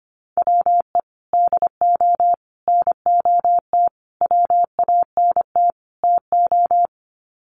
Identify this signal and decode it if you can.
Morse code, heard as a keyed tone, about 25 wpm, 700 Hz. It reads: WE DO NOT WANT TO
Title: Sets of 5 Words 25wpm